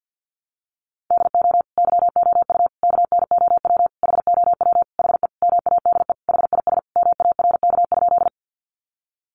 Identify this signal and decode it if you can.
Morse code, heard as a keyed tone, about 50 wpm, 700 Hz. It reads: DO YOU KNOW HOW HE MADE HIS MARK?
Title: Sentences from Top 300 Words 50wpm